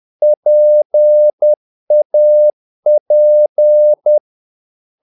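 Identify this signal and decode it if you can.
Morse code, heard as a keyed tone, about 10 wpm, 600 Hz. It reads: PAP
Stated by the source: Morse practice of short Danish words at 10 wpm, from CW med Gnister.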